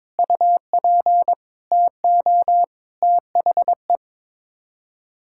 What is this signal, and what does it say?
Morse code, heard as a keyed tone, about 22 wpm, 700 Hz. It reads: UP TO THE
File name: practice / 22wpm / Sets of 7 Words - 22wpm